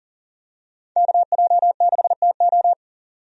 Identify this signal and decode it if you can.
Morse code, heard as a keyed tone, about 40 wpm, 700 Hz. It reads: KJ6TO
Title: USA Call Signs - 5 Characters - 40wpm